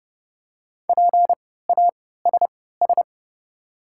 Morse code, heard as a keyed tone, about 30 wpm, spaced as for 17 wpm, 700 Hz. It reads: PASS
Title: 3x Character Spacing - Top 500 Words - 30wpm